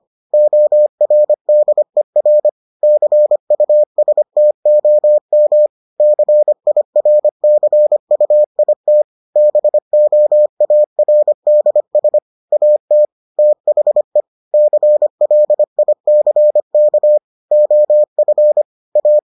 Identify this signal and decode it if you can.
Morse code, heard as a keyed tone, about 25 wpm, 600 Hz. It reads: ORDER CUSTOM CIRCUIT BOARDS AT THE CLICK OF A